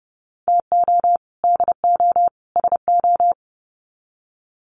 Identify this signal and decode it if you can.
Morse code, heard as a keyed tone, about 30 wpm, 700 Hz. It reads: TO DO SO
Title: Sets of 3 Words 30wpm